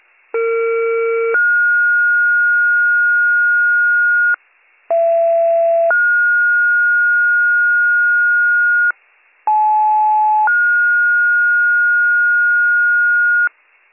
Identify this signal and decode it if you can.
Two-tone page, heard as a one-second tone, then a three-second tone, about 460 then 1430 Hz, about 670 then 1430 Hz, about 860 then 1430 Hz. It is heard from 0.3 to 13.5 s.